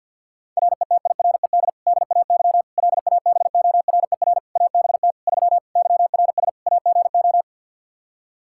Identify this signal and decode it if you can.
Morse code, heard as a keyed tone, about 48 wpm, 700 Hz. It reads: RETIRED DAY LABORER ABT 3 YRS AGO
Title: Practice QSO - Exchange 3 - More Info - 50wpm